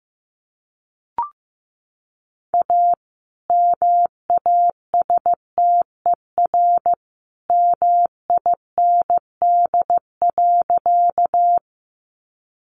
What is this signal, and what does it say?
Morse code, heard as a keyed tone, about 15 wpm, 700 Hz. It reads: A MASTER MIND.
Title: Sentences from Top 1300 Words 15wpm